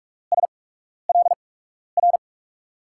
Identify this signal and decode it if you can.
Morse code, heard as a keyed tone, about 46 wpm, spaced as for 12 wpm, 700 Hz. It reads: SLR
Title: USA Call Signs - 6 Characters - 8x Character Spacing - 45wpm